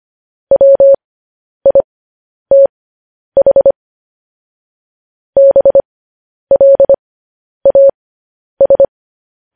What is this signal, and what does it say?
Morse code, heard as a keyed tone, about 25 wpm, spaced as for 10 wpm, 550 Hz. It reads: WITH BLAS